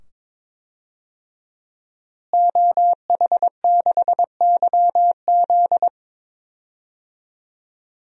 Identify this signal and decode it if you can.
Morse code, heard as a keyed tone, about 22 wpm, 700 Hz. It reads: OH6YZ
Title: DXCC-100 Call Signs - 5 Characters - Repeated 2x - 22wpm